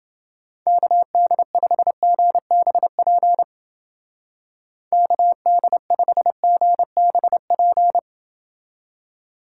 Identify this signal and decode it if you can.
Morse code, heard as a keyed tone, about 28 wpm, 700 Hz. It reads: KD5GBP KD5GBP